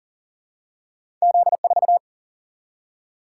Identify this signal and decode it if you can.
Morse code, heard as a keyed tone, about 38 wpm, 700 Hz. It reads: Z4
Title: Single-Letter Single-Number 40wpm